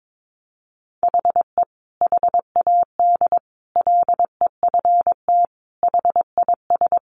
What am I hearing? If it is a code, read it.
Morse code, heard as a keyed tone, about 22 wpm, 700 Hz. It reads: HE HAD LEFT HIS
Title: Sentences from Top 1500 Words 22wpm